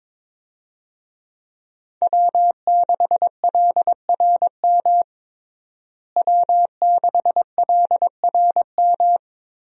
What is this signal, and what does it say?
Morse code, heard as a keyed tone, about 22 wpm, 700 Hz. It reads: W6LRM W6LRM
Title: USA Call Signs - 5 Characters - Repeated 3x - 22wpm